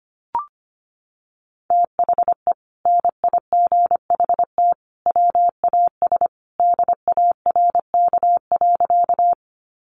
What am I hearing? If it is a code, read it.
Morse code, heard as a keyed tone, about 25 wpm, 700 Hz. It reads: THE NIGHT WAS DARK.
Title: Sentences from Top 500 Words 25wpm